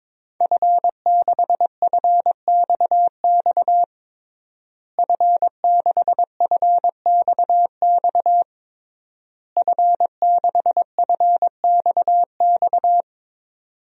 Morse code, heard as a keyed tone, about 22 wpm, 700 Hz. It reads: F6FXX F6FXX F6FXX
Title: DXCC-100 Call Signs - 5 Characters - Repeated 3x - 22wpm